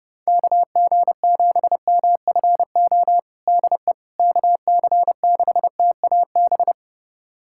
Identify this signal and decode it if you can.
Morse code, heard as a keyed tone, about 30 wpm, 700 Hz. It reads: KG7MFO DE KC6TAB
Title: USA Callsign DE Callsign 30wpm